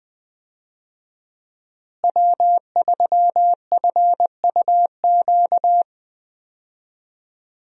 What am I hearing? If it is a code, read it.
Morse code, heard as a keyed tone, about 20 wpm, 700 Hz. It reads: W3FUQ
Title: USA Call Signs - 5 Characters - Repeated 2x - 20wpm